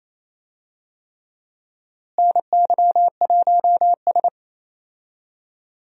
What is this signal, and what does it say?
Morse code, heard as a keyed tone, about 27 wpm, 700 Hz. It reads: NY1S
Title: USA Call Signs - 4 Characters - Repeated 3x - 28wpm